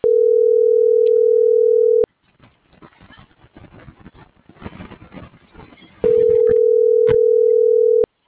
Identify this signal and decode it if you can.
Ringback tone (North America), 440 + 480 Hz.